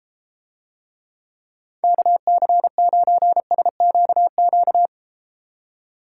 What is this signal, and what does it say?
Morse code, heard as a keyed tone, about 32 wpm, 700 Hz. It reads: KC9SQQ